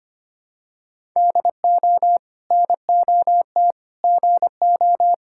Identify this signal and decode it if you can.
Morse code, heard as a keyed tone, about 25 wpm, 700 Hz. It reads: DO NOT GO
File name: practice / 25wpm / Sentences with 2-Word Phrases - 25wpm